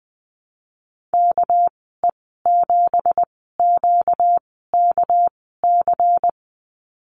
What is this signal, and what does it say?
Morse code, heard as a keyed tone, about 20 wpm, spaced as for 14 wpm, 700 Hz. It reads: KE7QKC
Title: 245 Lesson 39g - Letter z - Call Signs 20wpm